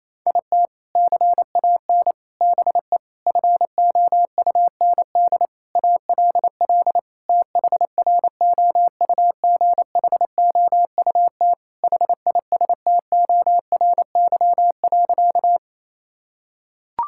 Morse code, heard as a keyed tone, about 28 wpm, 700 Hz. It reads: IT CAN BE FOUND ALL THROUGHOUT HISTORY.
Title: Sentences from Top 1000 Words - Review Part 5 - 28wpm